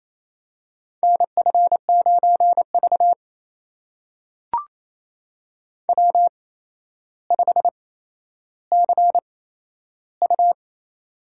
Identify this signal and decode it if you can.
Morse code, heard as a keyed tone, about 27 wpm, spaced as for 12 wpm, 700 Hz. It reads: NF9V W5CU
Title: USA Call Signs - 4 Characters - 8x Character Spacing - 28wpm